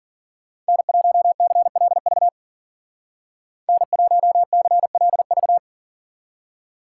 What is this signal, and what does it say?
Morse code, heard as a keyed tone, about 51 wpm, 700 Hz. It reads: N1CLV N1CLV